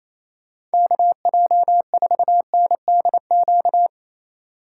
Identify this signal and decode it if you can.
Morse code, heard as a keyed tone, about 34 wpm, 700 Hz. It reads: KJ4NDQ
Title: Speed Racing - USA Call Signs 28wpm